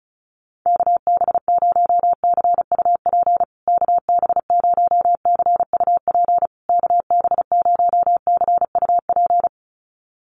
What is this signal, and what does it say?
Morse code, heard as a keyed tone, about 34 wpm, 700 Hz. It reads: KB0CUP KB0CUP KB0CUP
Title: USA Call Signs Repeated 3x 35wpm